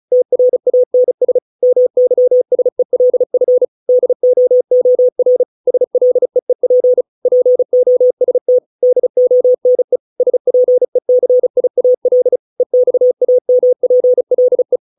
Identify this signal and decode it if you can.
Morse code, heard as a keyed tone, about 34 wpm, 500 Hz. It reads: TRANS MYSELF DOOR SLEEP POST DONE SPECIAL EXAMPLE